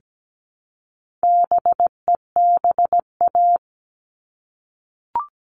Morse code, heard as a keyed tone, about 17 wpm, 700 Hz. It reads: BEBA
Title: All Cut Numbers - 4-Digit 17wpm